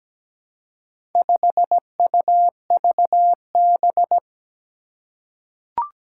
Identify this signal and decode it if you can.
Morse code, heard as a keyed tone, about 17 wpm, 700 Hz. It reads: 5UVB
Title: Four Challenging Letter-Number Combinations 17wpm